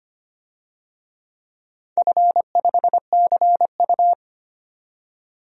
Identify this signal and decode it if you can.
Morse code, heard as a keyed tone, about 25 wpm, 700 Hz. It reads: F5CU
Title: DXCC-100 Call Signs - 4 Characters - 25wpm